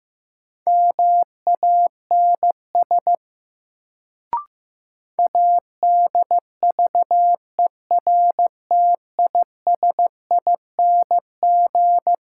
Morse code, heard as a keyed tone, about 16 wpm, 700 Hz. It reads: MANS ADVERTISING
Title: Sentences from Top 1000 Words - Review Part 6 - New Word Drill - 15wpm